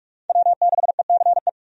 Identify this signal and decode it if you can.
Morse code, heard as a keyed tone, about 43 wpm, 700 Hz. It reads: W6ECE
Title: USA Call Signs - 5 Characters - Repeated 2x - 45wpm